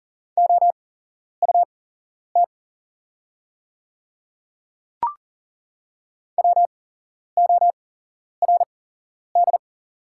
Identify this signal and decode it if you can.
Morse code, heard as a keyed tone, about 38 wpm, spaced as for 11 wpm, 700 Hz. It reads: OUT WORD
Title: 8x Character Spacing - Top 100 Words - 40wpm